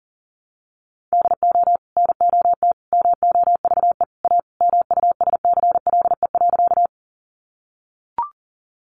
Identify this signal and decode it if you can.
Morse code, heard as a keyed tone, about 40 wpm, 700 Hz. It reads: DO NOT MOVE A MUSCLE.
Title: Sentences from Top 1300 Words 40wpm